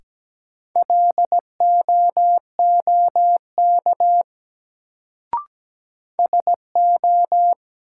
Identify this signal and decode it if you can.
Morse code, heard as a keyed tone, about 17 wpm, 700 Hz. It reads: LOOK SO
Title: Sentences from Top 100 Words - New Word Drill - 17wpm